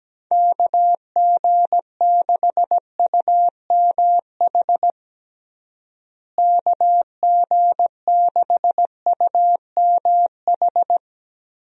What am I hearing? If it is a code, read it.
Morse code, heard as a keyed tone, about 17 wpm, 700 Hz. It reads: KG6UMH KG6UMH
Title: USA Call Signs - 6 Characters - Repeated 2x - 17wpm